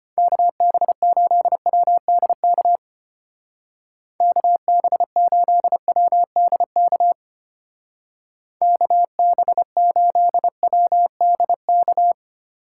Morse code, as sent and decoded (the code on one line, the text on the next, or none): -.- -... ---.. .-- -.. -.- / -.- -... ---.. .-- -.. -.- / -.- -... ---.. .-- -.. -.-
KB8WDK KB8WDK KB8WDK